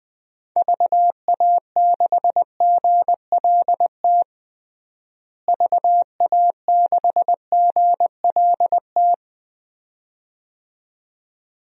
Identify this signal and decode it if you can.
Morse code, heard as a keyed tone, about 20 wpm, 700 Hz. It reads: VA6GLT VA6GLT
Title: DXCC-100 Call Signs - 6 Characters - Repeated 2x - 20wpm